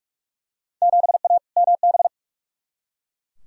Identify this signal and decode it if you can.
Morse code, heard as a keyed tone, about 41 wpm, 700 Hz. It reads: 7A MB
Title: Field Day Exchanges Encore - 45wpm